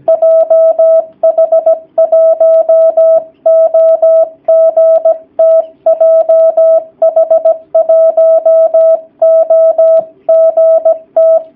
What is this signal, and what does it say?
Morse code, heard as a keyed tone, about 17 wpm, 650 Hz. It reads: JH1OGT JH1OGT
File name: jh8cbh / 2019cw / sp16- w